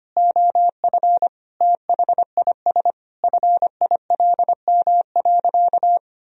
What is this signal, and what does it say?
Morse code, heard as a keyed tone, about 25 wpm, 700 Hz. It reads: OF THIS FILM.
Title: Sentences from Top 1000 Words - Review Part 3 - 25wpm